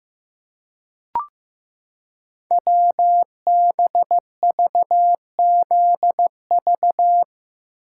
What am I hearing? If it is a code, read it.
Morse code, heard as a keyed tone, about 15 wpm, 700 Hz. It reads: WBVZV
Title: Code Groups - 5 Letters - 15wpm